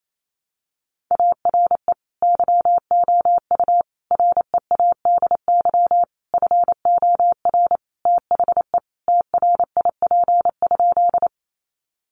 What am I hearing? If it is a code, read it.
Morse code, heard as a keyed tone, about 28 wpm, 700 Hz. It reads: ARE YOU READY FOR THE TRIP?
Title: Sentences from Top 1300 Words 28wpm